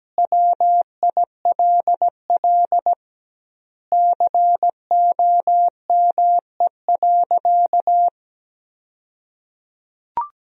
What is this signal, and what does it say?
Morse code, heard as a keyed tone, about 17 wpm, 700 Hz. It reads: WILL COME.
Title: 2x Word-Spacing - Sentences from Top 100 Words - 17wpm